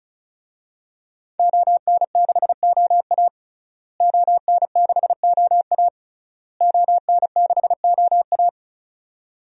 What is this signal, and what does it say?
Morse code, heard as a keyed tone, about 36 wpm, 700 Hz. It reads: ON6OA ON6OA ON6OA
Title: DXCC-100 Call Signs - 5 Characters - Repeated 3x - 35wpm